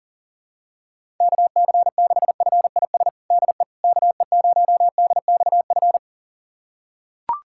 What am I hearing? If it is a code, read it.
Morse code, heard as a keyed tone, about 40 wpm, 700 Hz. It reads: KC6FIS DE KE0DXF
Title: USA Callsign DE Callsign - 6 Characters - 40wpm